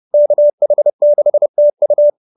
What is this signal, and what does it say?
Morse code, heard as a keyed tone, about 30 wpm, 590 Hz. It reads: KH6TU
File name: QSO309 30